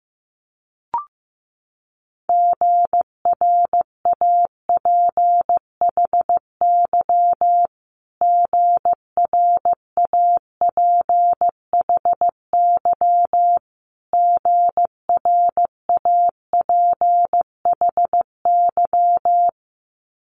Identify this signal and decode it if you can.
Morse code, heard as a keyed tone, about 15 wpm, 700 Hz. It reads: GRAPHY GRAPHY GRAPHY
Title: Multi-letter Phonemes Repeated 3x 15wpm